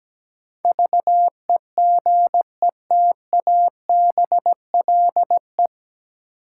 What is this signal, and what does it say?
Morse code, heard as a keyed tone, about 17 wpm, 700 Hz. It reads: VEGETABLE
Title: Sentences from Top 1300 Words - New Word Drill - 17wpm